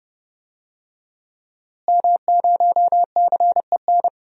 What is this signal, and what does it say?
Morse code, heard as a keyed tone, about 30 wpm, 700 Hz. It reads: M0CEN
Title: DXCC-100 Call Signs - 5 Characters - Repeated 2x - 30wpm